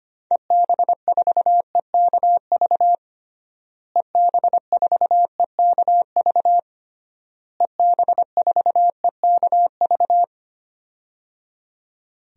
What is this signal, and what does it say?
Morse code, heard as a keyed tone, about 25 wpm, 700 Hz. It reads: EB4EKV EB4EKV EB4EKV
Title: DXCC-100 Call Signs - 6 Characters - Repeated 3x - 25wpm